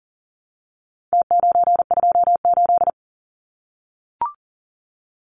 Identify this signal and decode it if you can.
Morse code, heard as a keyed tone, about 40 wpm, 700 Hz. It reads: T928
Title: Numbers - Contest Style - 4-Digit 40wpm